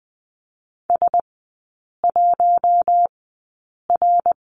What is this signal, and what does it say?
Morse code, heard as a keyed tone, about 20 wpm, spaced as for 8 wpm, 700 Hz. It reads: S1R
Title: 074 Lesson 13e - Letter L - Three Characters 20wpm